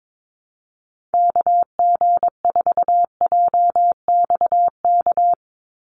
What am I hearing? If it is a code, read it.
Morse code, heard as a keyed tone, about 22 wpm, 700 Hz. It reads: KG4JXK
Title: USA Call Signs 22wpm